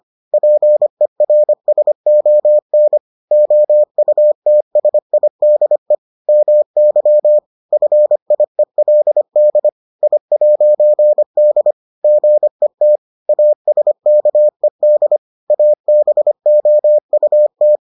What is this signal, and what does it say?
Morse code, heard as a keyed tone, about 25 wpm, 600 Hz. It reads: PERSON OUTSIDE MY FIELD I'D GET ASKED ABOUT